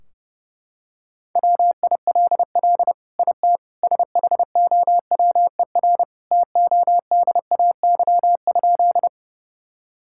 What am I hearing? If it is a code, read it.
Morse code, heard as a keyed tone, about 30 wpm, 700 Hz. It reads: WILL IT SHOWER TODAY?
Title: Sentences from Top 1600 Words 30wpm